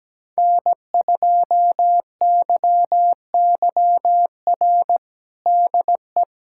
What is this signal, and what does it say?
Morse code, heard as a keyed tone, about 17 wpm, 700 Hz. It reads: N2YYR DE
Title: USA Callsign DE Callsign - 5 Characters - 17wpm